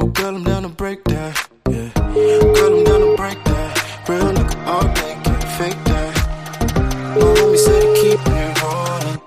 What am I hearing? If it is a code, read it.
Ringback tone (Europe, 425 Hz), with noise.